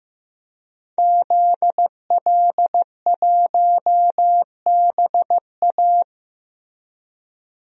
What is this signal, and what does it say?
Morse code, heard as a keyed tone, about 15 wpm, 700 Hz. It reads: ZL1BA
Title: DXCC-100 Call Signs - 5 Characters - Repeated 2x - 15wpm